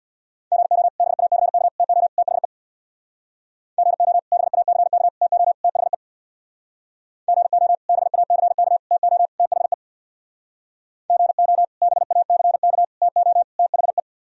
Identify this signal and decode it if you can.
Morse code, heard as a keyed tone, about 62 wpm, 700 Hz. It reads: GO BACK TO THE GO BACK TO THE GO BACK TO THE GO BACK TO THE
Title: Speed-Racing - Sets of 4 Words 50wpm